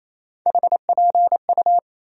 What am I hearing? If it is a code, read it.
Morse code, heard as a keyed tone, about 28 wpm, 700 Hz. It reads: HPU
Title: USA Call Sign Postfixes - 3 Letters 28wpm